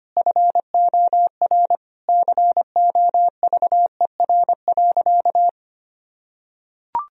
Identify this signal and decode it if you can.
Morse code, heard as a keyed tone, about 25 wpm, 700 Hz. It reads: FOR COVER.
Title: Sentences from Top 1000 Words - Review Part 1 - 25wpm